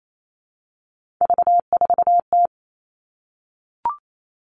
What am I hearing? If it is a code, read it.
Morse code, heard as a keyed tone, about 28 wpm, 700 Hz. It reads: V4T
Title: Three Challenging Letter-Number Combinations 28wpm